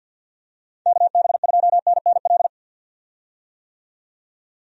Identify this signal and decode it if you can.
Morse code, heard as a keyed tone, about 52 wpm, 700 Hz. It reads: KB2NNL
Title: USA Call Signs - 6 Characters - 50wpm